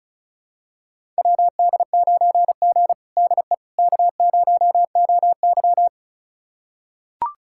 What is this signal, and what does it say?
Morse code, heard as a keyed tone, about 36 wpm, 700 Hz. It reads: WD9G DE K0OY
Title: USA Callsign DE Callsign - 4 Characters - 35wpm